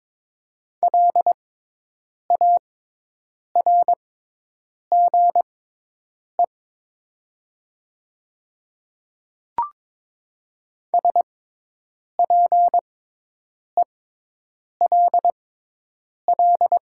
Morse code, heard as a keyed tone, about 22 wpm, spaced as for 8 wpm, 700 Hz. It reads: LARGE SPELL
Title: 6x Character Spacing - Top 200 Words - 22wpm